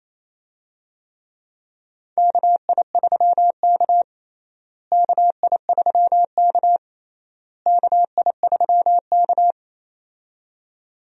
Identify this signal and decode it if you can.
Morse code, heard as a keyed tone, about 28 wpm, 700 Hz. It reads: KI3K KI3K KI3K
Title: USA Call Signs - 4 Characters - Repeated 3x - 28wpm